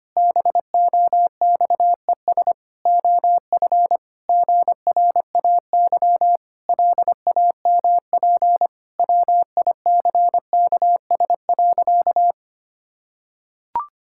Morse code, heard as a keyed tone, about 25 wpm, 700 Hz. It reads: BOXES OF GRAY LAMP WICKS.
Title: Pangram Challenge v2 - 25wpm